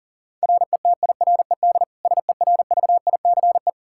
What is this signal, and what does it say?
Morse code, heard as a keyed tone, about 37 wpm, 700 Hz. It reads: RETIRED SERVICE
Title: Practice QSO - Exchange 3 - More Info - 40wpm